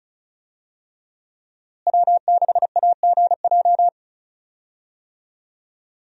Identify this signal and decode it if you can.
Morse code, heard as a keyed tone, about 36 wpm, 700 Hz. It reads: W6AGJ
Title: USA Call Signs - 5 Characters - Repeated 3x - 35wpm